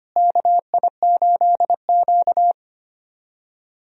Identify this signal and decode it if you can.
Morse code, heard as a keyed tone, about 25 wpm, 700 Hz. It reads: KI8Q
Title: USA Call Signs - 4 Characters - Repeated 2x - 25wpm